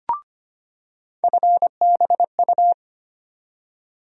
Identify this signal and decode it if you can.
Morse code, heard as a keyed tone, about 25 wpm, 700 Hz. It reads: FBU
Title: USA Call Sign Postfixes - 3 Letters 25wpm